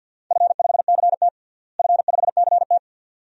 Morse code, heard as a keyed tone, about 50 wpm, 700 Hz. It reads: F5CT F5CT
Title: DXCC-100 Call Signs - 4 Characters - Repeated 2x - 50wpm